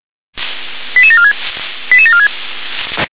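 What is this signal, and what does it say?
Five-tone selective call (ZVEI), heard as quick runs of five tones, tones of about 70 ms. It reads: 88746 88746
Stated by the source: ZVEI five-tone selective call received on an SDR.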